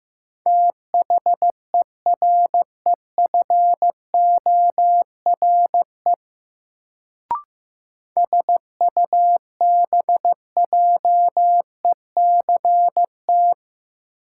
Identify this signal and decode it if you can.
Morse code, heard as a keyed tone, about 15 wpm, 700 Hz. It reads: THEREFORE SUBJECT
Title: Sentences from Top 800 Words - New Word Drill - 15wpm